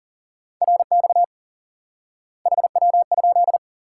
Morse code, heard as a keyed tone, about 40 wpm, 700 Hz. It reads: RX HW?
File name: IQE Course 3 - Lesson 89 - YAGI - Two Elements - 40wpm